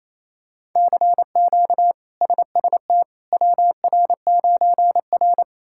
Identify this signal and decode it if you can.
Morse code, heard as a keyed tone, about 28 wpm, 700 Hz. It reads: CQ SST WR9R
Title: K1USN SST QSO Practice - 28wpm